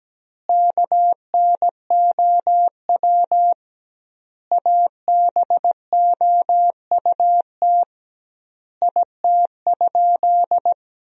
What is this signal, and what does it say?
Morse code, heard as a keyed tone, about 17 wpm, 700 Hz. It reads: KNOW ABOUT IT?
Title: 2x Word-Spacing - Sentences from Top 100 Words - 17wpm